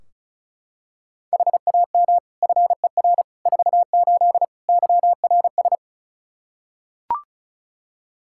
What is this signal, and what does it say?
Morse code, heard as a keyed tone, about 36 wpm, 700 Hz. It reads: HAM FER 48 YRS
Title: K1USN SST QSO Practice 35wpm